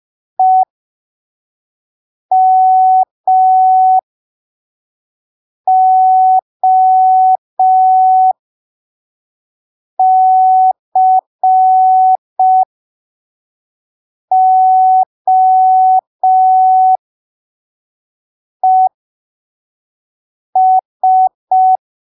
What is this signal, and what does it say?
Morse code, heard as a keyed tone, about 5 wpm, spaced as for 3 wpm, 750 Hz. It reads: EMOCOES
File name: arquivos / texto / texto 18 05ppm